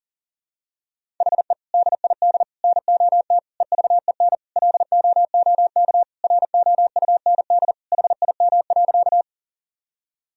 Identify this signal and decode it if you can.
Morse code, heard as a keyed tone, about 40 wpm, 700 Hz. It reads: HE DID NOT EVEN LOOK ROUND HIM.
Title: Sentences from Top 1000 Words - Review Part 1 - 40wpm